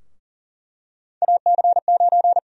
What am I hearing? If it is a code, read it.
Morse code, heard as a keyed tone, about 40 wpm, 700 Hz. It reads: AC9
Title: USA Call Sign Prefixes - Repeated 2x - 40wpm